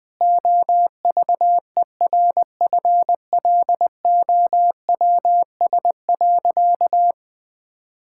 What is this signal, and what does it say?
Morse code, heard as a keyed tone, about 21 wpm, 700 Hz. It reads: OVERFLOWS.
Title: Pangram Challenge v2 - 20wpm